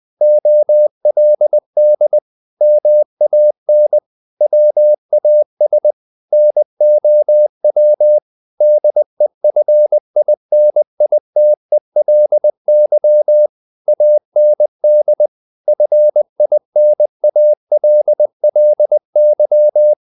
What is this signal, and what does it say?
Morse code, heard as a keyed tone, about 20 wpm, 600 Hz. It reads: OLD MAN WAS NOW DEFINITELY AND FINALLY
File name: the old man and the sea / the old man and the sea part 1 20wpm